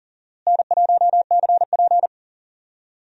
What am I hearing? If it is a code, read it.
Morse code, heard as a keyed tone, about 37 wpm, 700 Hz. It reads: N1CP
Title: USA Call Signs - 4 Characters - Repeated 2x - 40wpm